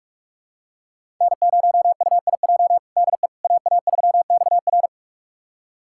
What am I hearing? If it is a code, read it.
Morse code, heard as a keyed tone, about 46 wpm, 700 Hz. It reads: N0UIJ DE AA3XR
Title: USA Callsign DE Callsign - 5 Characters - 45wpm